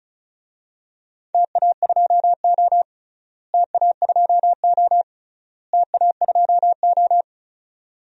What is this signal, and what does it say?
Morse code, heard as a keyed tone, about 36 wpm, 700 Hz. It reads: TA2O TA2O TA2O
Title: DXCC-100 Call Signs - 4 Characters - Repeated 3x - 35wpm